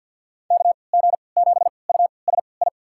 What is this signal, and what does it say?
Morse code, heard as a keyed tone, about 46 wpm, spaced as for 27 wpm, 690 Hz. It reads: KG7USI
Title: USA Call Signs - 6 Characters - 3x Character Spacing - 50wpm